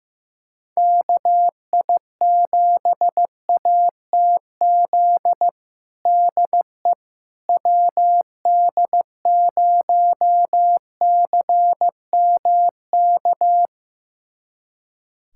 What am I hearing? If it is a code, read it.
Morse code, heard as a keyed tone, about 15 wpm, 700 Hz. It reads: KI7ATZ DE WD0CMK
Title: USA Callsign DE Callsign - 6 Characters - 15wpm